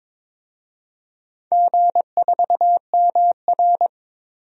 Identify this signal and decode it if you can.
Morse code, heard as a keyed tone, about 23 wpm, 700 Hz. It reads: G4MR
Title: DXCC-100 Call Signs - 4 Characters - 22wpm